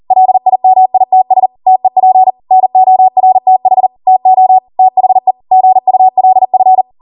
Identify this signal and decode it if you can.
Morse code, heard as a keyed tone, about 40 wpm, 750 Hz. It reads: LIMITS TEP NORTH TO THE GULF